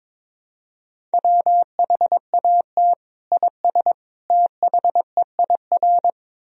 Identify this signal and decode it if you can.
Morse code, heard as a keyed tone, about 21 wpm, 700 Hz. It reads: WHAT IS THEIR
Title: Sentences from Top 1000 Words - Review Part 1 - 22wpm